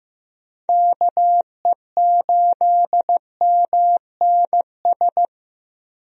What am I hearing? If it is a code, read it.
Morse code, heard as a keyed tone, about 15 wpm, 700 Hz. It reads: KE8MNS
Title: USA Call Signs - 6 Characters - Repeated 3x - 15wpm